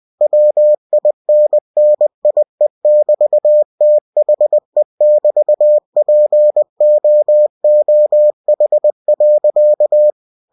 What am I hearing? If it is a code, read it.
Morse code, heard as a keyed tone, about 20 wpm, 600 Hz. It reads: WINNIE<BT>THE<BT>POOH.
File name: winnie-20wpm